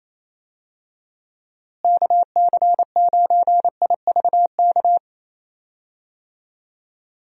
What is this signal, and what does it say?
Morse code, heard as a keyed tone, about 28 wpm, 700 Hz. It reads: KC9IVK
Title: USA Call Signs - 6 Characters - Repeated 3x - 28wpm